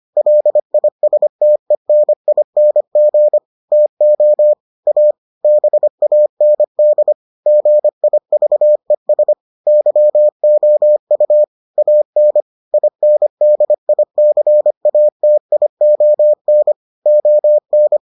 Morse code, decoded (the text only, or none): LISTENING TO A BAND GIVES YOU AN INDICATION ON